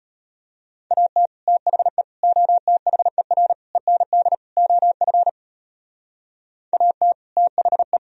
Morse code, heard as a keyed tone, about 30 wpm, 700 Hz. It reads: AT THE OTHER END OF AT THE
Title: Speed-Racing - Sets of 5 Words 25wpm